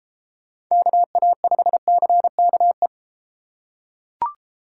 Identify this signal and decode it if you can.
Morse code, heard as a keyed tone, about 31 wpm, 700 Hz. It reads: KA5CKE